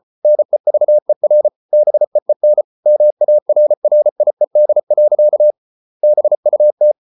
Morse code, heard as a keyed tone, about 34 wpm, 600 Hz. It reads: NEVER BEEN MARRIED. BUT